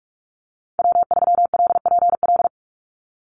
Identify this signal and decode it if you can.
Morse code, heard as a keyed tone, about 44 wpm, 700 Hz. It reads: W3LPL
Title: Call Signs Repeated 3x - 45wpm